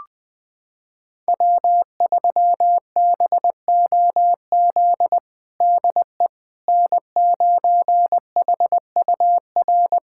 Morse code, heard as a keyed tone, about 21 wpm, 700 Hz. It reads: W3BOZ DE N9HUR
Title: USA Callsign DE Callsign - 5 Characters - 20wpm